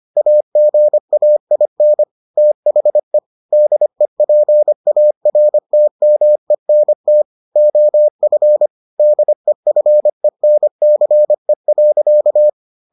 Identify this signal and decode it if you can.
Morse code, heard as a keyed tone, about 25 wpm, 600 Hz. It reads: AGAIN THE DEPARTMENT OF DEFENCE.